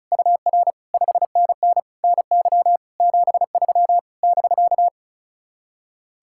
Morse code, heard as a keyed tone, about 44 wpm, 700 Hz. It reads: UR 5NN NY 73 <BK>
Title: Speed-Racing Common QSO Phrases Encore - 35wpm